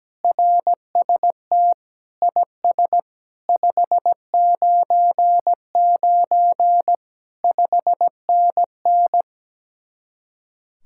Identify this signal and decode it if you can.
Morse code, heard as a keyed tone, about 17 wpm, 700 Hz. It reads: RST IS 599 5NN
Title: Speed-Racing Common QSO Phrases Encore - 17wpm